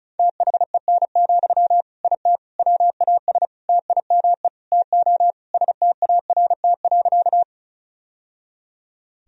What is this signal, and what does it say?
Morse code, heard as a keyed tone, about 34 wpm, 700 Hz. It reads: THEN, IT WAS TIME TO START.
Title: Sentences from Top 1000 Words - Review Part 1 - 35wpm